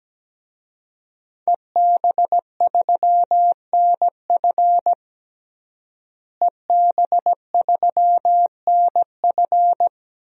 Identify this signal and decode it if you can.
Morse code, heard as a keyed tone, about 17 wpm, 700 Hz. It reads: EB3NF EB3NF
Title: DXCC-100 Call Signs - 5 Characters - Repeated 3x - 17wpm